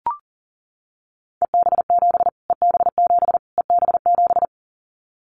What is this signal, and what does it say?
Morse code, heard as a keyed tone, about 37 wpm, 700 Hz. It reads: EB7 EB7 EB7
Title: Callsign Prefixes for Top 25 DXCC - Repeated 3x - 40wpm